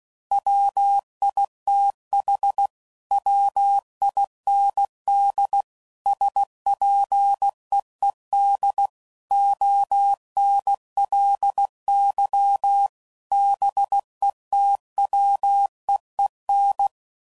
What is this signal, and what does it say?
Morse code, heard as a keyed tone, about 16 wpm, 800 Hz. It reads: WITH WIND SPEED ONLY BETWEEN
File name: CW Code Groups @15WPM / Weather Sets / Weather 5 1